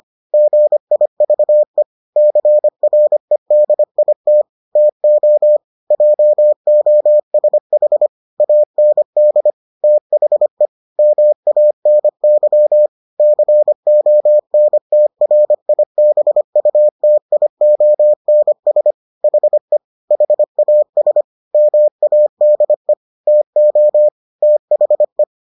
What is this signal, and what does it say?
Morse code, heard as a keyed tone, about 25 wpm, 600 Hz. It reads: GIVE CREDIT TO JOSH AND THE MANY CONTRIBUTIONS HE HAS MADE TO THE